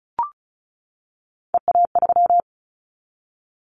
Morse code, heard as a keyed tone, about 36 wpm, 700 Hz. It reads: EA3
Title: Callsign Prefixes for Top 25 DXCC - 35wpm